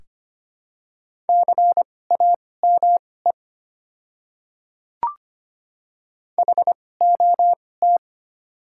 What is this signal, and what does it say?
Morse code, heard as a keyed tone, about 25 wpm, spaced as for 18 wpm, 700 Hz. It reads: CAME HOT
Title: 2x Character Spacing - Top 300 Words - 25wpm